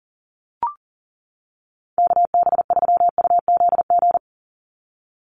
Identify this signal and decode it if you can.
Morse code, heard as a keyed tone, about 40 wpm, 700 Hz. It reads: KB3UZG
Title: USA Call Signs 40wpm